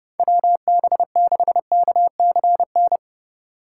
Morse code, heard as a keyed tone, about 30 wpm, 700 Hz. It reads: WB6KCN
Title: USA Call Signs - 6 Characters - 4x Character Spacing - 30wpm